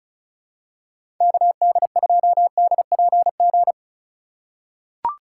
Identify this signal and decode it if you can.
Morse code, heard as a keyed tone, about 36 wpm, 700 Hz. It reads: KD2DPG
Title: USA Call Signs - 6 Characters - 4x Character Spacing - 35wpm